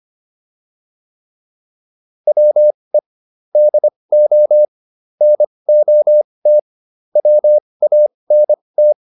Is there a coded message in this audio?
Yes: Morse code from 2.3 to 8.9 s, heard as a keyed tone, about 25 wpm, spaced as for 20 wpm, 600 Hz. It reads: WE DO NOT WANT